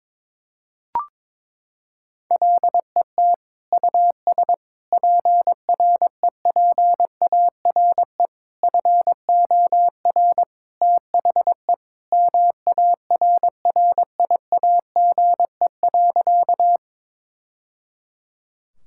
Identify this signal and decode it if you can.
Morse code, heard as a keyed tone, about 22 wpm, 700 Hz. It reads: LET US PREPARE FOR THE MARRIAGE.
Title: Sentences from Top 1000 Words - Review Part 6 - 22wpm